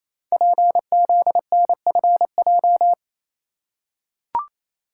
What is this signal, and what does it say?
Morse code, heard as a keyed tone, about 29 wpm, 700 Hz. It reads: PZNFJ
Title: Code Groups - 5 Letters - 28wpm